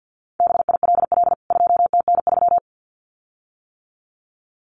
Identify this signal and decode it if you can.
Morse code, heard as a keyed tone, about 48 wpm, 700 Hz. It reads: BILL 2TN3
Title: CWT Exchange 50wpm